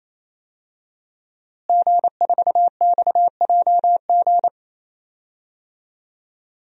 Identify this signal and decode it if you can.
Morse code, heard as a keyed tone, about 28 wpm, 690 Hz. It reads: G4XJG
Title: DXCC-100 Call Signs - 5 Characters - Repeated 3x - 28wpm